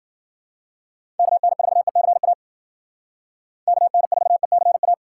Morse code, heard as a keyed tone, about 52 wpm, 700 Hz. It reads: KN4ECA KN4ECA